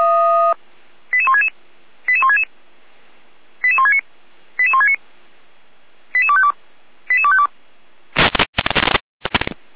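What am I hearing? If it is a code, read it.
Five-tone selective call (ZVEI), heard as quick runs of five tones, tones of about 70 ms. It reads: 88166 88166 88169 88169 88252 88252
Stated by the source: ZVEI five-tone selective call 88247, received on an SDR, from the Feuerwehr Orlamünde fire brigade audio server.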